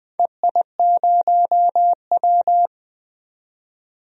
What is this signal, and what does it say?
Morse code, heard as a keyed tone, about 20 wpm, 700 Hz. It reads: EI0W
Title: DXCC-100 Call Signs - 4 Characters - 20wpm